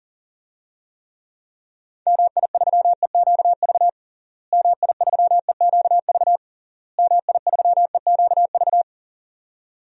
Morse code, heard as a keyed tone, about 40 wpm, 700 Hz. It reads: MI3EQV MI3EQV MI3EQV
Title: DXCC-100 Call Signs - 6 Characters - Repeated 3x - 40wpm